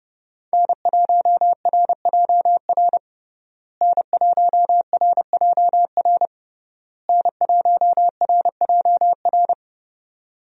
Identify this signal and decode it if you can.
Morse code, heard as a keyed tone, about 30 wpm, 700 Hz. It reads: N1RJR N1RJR N1RJR
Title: USA Call Signs - 5 Characters - Repeated 3x - 30wpm